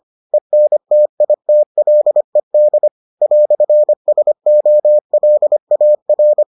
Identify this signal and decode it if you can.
Morse code, heard as a keyed tone, about 25 wpm, 600 Hz. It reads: ENTITLED "SOLAR